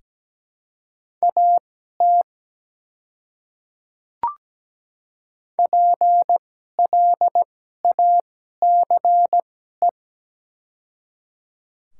Morse code, heard as a keyed tone, about 17 wpm, spaced as for 12 wpm, 700 Hz. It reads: AT PLACE
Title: 2x Character Spacing - Top 200 Words - 17wpm